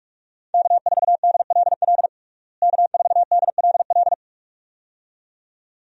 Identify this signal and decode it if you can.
Morse code, heard as a keyed tone, about 42 wpm, 700 Hz. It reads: K4DLL K4DLL
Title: USA Call Signs - 5 Characters - Repeated 2x - 45wpm